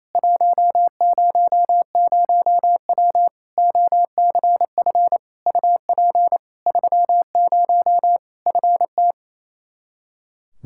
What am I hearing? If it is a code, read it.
Morse code, heard as a keyed tone, about 28 wpm, 700 Hz. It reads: 100W OCF UP 30 FT
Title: Practice QSO - Exchange 2 - Info - 28wpm